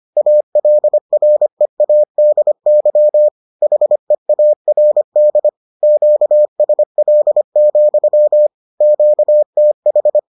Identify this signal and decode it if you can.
Morse code, heard as a keyed tone, about 25 wpm, 600 Hz. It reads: ALREADY HEARD QSL, QTH